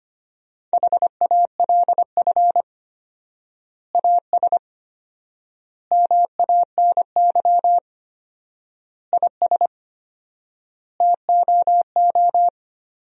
Morse code, heard as a keyed tone, about 25 wpm, 700 Hz. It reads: HALF AS MANY IS TOO
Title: 4x Word-Spacing - Sentences from Top 400 Words - 25wpm